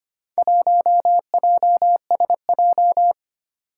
Morse code, heard as a keyed tone, about 25 wpm, 700 Hz. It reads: 1JSJ
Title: Four Challenging Letter-Number Combinations 25wpm